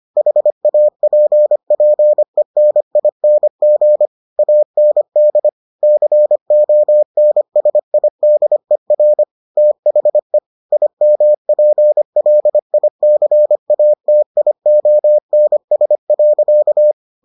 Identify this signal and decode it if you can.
Morse code, heard as a keyed tone, about 25 wpm, 600 Hz. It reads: HAPPENING AND CONSIDER THE IMPLICATIONS.